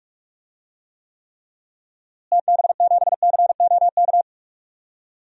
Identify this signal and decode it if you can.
Morse code, heard as a keyed tone, about 42 wpm, 700 Hz. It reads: TB7COK
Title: DXCC-100 Call Signs - 6 Characters - Repeated 3x - 45wpm